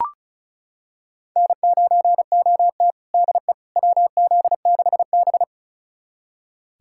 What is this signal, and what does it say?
Morse code, heard as a keyed tone, about 36 wpm, 690 Hz. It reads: N9OT DE WZ6B
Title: USA Callsign DE Callsign - 4 Characters - 35wpm